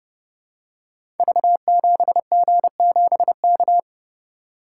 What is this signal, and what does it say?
Morse code, heard as a keyed tone, about 29 wpm, 700 Hz. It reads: V7G7K
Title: Code Groups - 5 Letter:Numbers - 30wpm